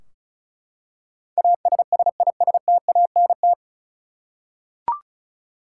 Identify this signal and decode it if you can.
Morse code, heard as a keyed tone, about 34 wpm, 700 Hz. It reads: ASSISTANT
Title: Practice QSO - Exchange 3 - More Info - 35wpm